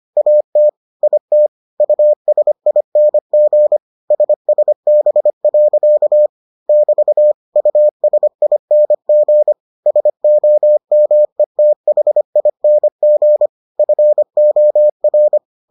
Morse code, heard as a keyed tone, about 25 wpm, 600 Hz. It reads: AT IT USING SSB. <BT> USING SOMETHING FOR